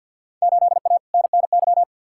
Morse code, heard as a keyed tone, about 48 wpm, 700 Hz. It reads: 8A NNY
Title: Field Day Exchanges Encore - 50wpm